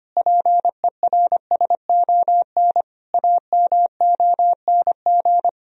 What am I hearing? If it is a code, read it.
Morse code, heard as a keyed tone, about 25 wpm, 700 Hz. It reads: PERSON AMONG
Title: Sentences from Top 1000 Words - Review Part 3 - 25wpm